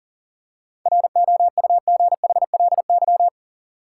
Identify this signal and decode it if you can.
Morse code, heard as a keyed tone, about 40 wpm, 700 Hz. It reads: ROUGHLY
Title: Seven Letter Words - 40wpm